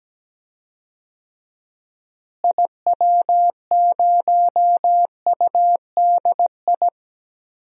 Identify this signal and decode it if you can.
Morse code, heard as a keyed tone, about 17 wpm, 700 Hz. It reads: IW0UDI
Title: DXCC-100 Call Signs - 6 Characters - Repeated 2x - 17wpm